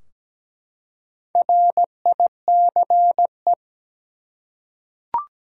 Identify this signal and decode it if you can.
Morse code, heard as a keyed tone, about 17 wpm, 700 Hz. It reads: RICE
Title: Four Letter Words - 17wpm